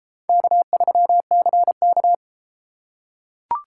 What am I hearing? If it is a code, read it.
Morse code, heard as a keyed tone, about 29 wpm, 700 Hz. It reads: K3CK